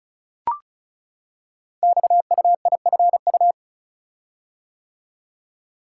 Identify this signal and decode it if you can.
Morse code, heard as a keyed tone, about 36 wpm, 700 Hz. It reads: XUIFU
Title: Code Groups - 5 Letters - 35wpm